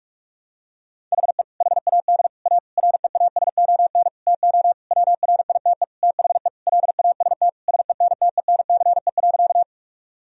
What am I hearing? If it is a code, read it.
Morse code, heard as a keyed tone, about 46 wpm, 700 Hz. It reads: HE HAD A REASON TO WRITE THE LAST SENTENCE.
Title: Sentences from Top 1000 Words - Review Part 1 - 45wpm